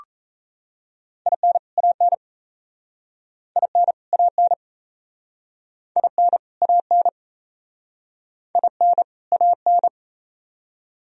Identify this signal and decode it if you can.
Morse code, heard as a keyed tone, about 37 wpm, 700 Hz. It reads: IN AN IN AN IN AN IN AN